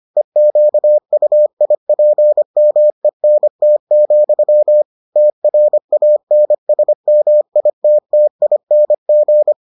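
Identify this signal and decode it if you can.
Morse code, heard as a keyed tone, about 25 wpm, 600 Hz. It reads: EQUIPMENT, TRANSMITTING
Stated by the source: Morse edition of the Foundations of Amateur Radio episodes.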